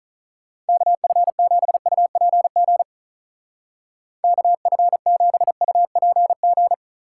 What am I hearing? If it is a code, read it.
Morse code, heard as a keyed tone, about 44 wpm, 700 Hz. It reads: KF7UPG KF7UPG
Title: Speed Racing - USA 2x3 Call Signs 35wpm